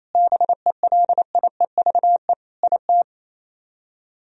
Morse code, heard as a keyed tone, about 34 wpm, 700 Hz. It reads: BELIEVE IT
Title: Speed-Racing - Sets of 5 Words 28wpm